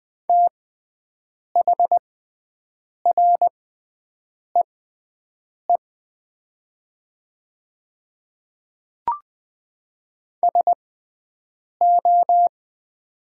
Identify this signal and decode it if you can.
Morse code, heard as a keyed tone, about 20 wpm, spaced as for 7 wpm, 700 Hz. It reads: THREE SO
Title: 6x Character Spacing - Top 200 Words - 20wpm